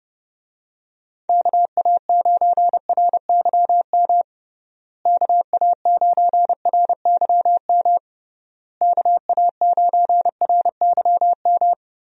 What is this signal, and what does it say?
Morse code, heard as a keyed tone, about 30 wpm, 700 Hz. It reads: KA9RYM KA9RYM KA9RYM
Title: USA Call Signs - 6 Characters - Repeated 3x - 30wpm